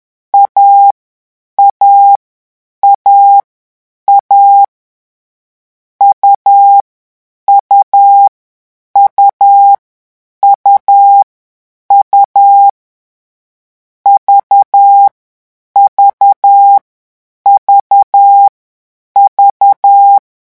Morse code, heard as a keyed tone, about 10 wpm, 800 Hz. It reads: A A A A U U U U U V V V V
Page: Leçon 3 - vitesse 10 mots minute